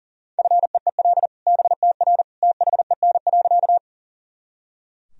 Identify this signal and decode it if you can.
Morse code, heard as a keyed tone, about 40 wpm, 700 Hz. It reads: FEEL BTR THEN.
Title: Sentences from Top 400 Words and Abbreviations 40wpm